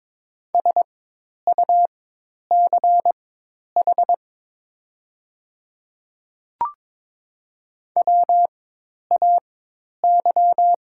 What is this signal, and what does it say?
Morse code, heard as a keyed tone, about 22 wpm, spaced as for 10 wpm, 700 Hz. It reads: SUCH WAY
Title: 4x Character Spacing - Top 300 Words - 22wpm